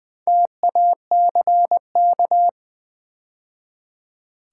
Morse code, heard as a keyed tone, about 20 wpm, 700 Hz. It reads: TACK
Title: Build A Word - 2 Letters At A Time - 6-Letter Words - 20wpm